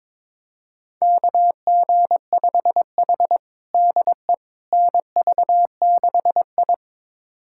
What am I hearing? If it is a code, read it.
Morse code, heard as a keyed tone, about 22 wpm, 700 Hz. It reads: KG5H DE NV6I
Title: USA Callsign DE Callsign - 4 Characters - 22wpm